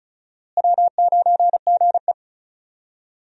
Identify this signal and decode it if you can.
Morse code, heard as a keyed tone, about 34 wpm, 700 Hz. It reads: W9GE
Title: USA Call Signs - 4 Characters - Repeated 2x - 35wpm